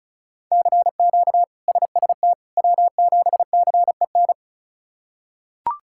Morse code, heard as a keyed tone, about 34 wpm, 700 Hz. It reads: CQ SST W7CEN
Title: K1USN SST QSO Practice - 35wpm